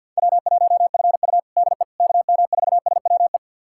Morse code, heard as a keyed tone, about 46 wpm, 700 Hz. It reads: W1FU DE KM4SWE
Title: USA Callsign DE Callsign 50wpm